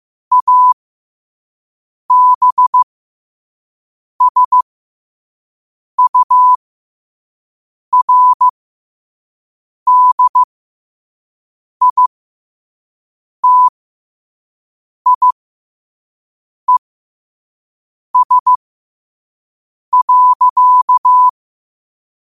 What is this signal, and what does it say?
Morse code, heard as a keyed tone, about 15 wpm, spaced as for 5 wpm, 1000 Hz. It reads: ABSURDITIES.
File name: qotdfiles / 5wpm